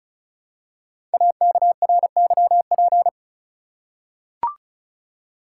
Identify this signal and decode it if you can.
Morse code, heard as a keyed tone, about 36 wpm, 700 Hz. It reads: AKRYP
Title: Code Groups - 5 Letters - 35wpm